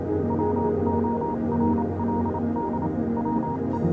Morse code, heard as a keyed tone, about 28 wpm, spaced as for 25 wpm, 940 Hz, with noise. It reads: WORLD WE
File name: Vylet Pony / aura / do you remember the song she sang (morse code version)